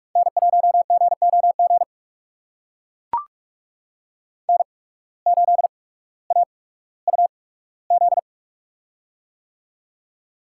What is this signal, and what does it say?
Morse code, heard as a keyed tone, about 44 wpm, spaced as for 19 wpm, 700 Hz. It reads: N1GOG N8AUZ
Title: USA Call Signs - 5 Characters - 8x Character Spacing - 45wpm